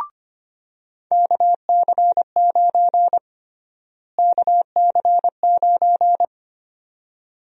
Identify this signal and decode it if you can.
Morse code, heard as a keyed tone, about 25 wpm, 700 Hz. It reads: KC9 KC9
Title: USA Call Sign Prefixes - Repeated 2x - 25wpm